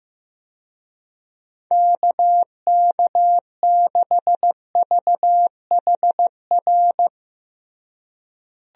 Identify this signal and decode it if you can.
Morse code, heard as a keyed tone, about 16 wpm, 700 Hz. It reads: KK6VHR
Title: USA Call Signs - 6 Characters - Repeated 3x - 15wpm